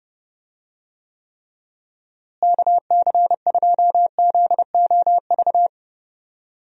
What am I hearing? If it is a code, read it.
Morse code, heard as a keyed tone, about 32 wpm, 700 Hz. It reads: KC2ZOV
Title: USA Call Signs - 6 Characters - Repeated 2x - 30wpm